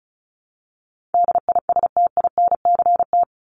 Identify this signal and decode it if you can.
Morse code, heard as a keyed tone, about 34 wpm, 700 Hz. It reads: DISTINCT
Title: Top 1501-1750 Words 35wpm